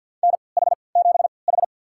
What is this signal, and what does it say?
Morse code, heard as a keyed tone, about 46 wpm, spaced as for 28 wpm, 700 Hz. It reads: NH7H
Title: USA Call Signs - 4 Characters - 3x Character Spacing - 50wpm